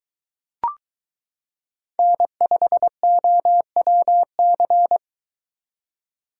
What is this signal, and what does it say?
Morse code, heard as a keyed tone, about 22 wpm, 700 Hz. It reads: N5OWC